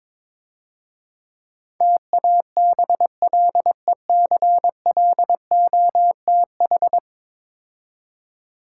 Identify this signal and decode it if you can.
Morse code, heard as a keyed tone, about 22 wpm, 700 Hz. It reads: TABLECLOTH
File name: Easy Words - Speed-Racing with Hints - 22wpm